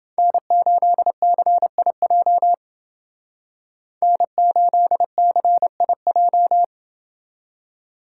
Morse code, heard as a keyed tone, about 28 wpm, 700 Hz. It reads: N8CIJ N8CIJ